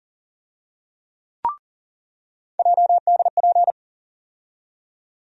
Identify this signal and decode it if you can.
Morse code, heard as a keyed tone, about 40 wpm, 700 Hz. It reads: JDP
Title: USA Call Sign Postfixes - 3 Letters 40wpm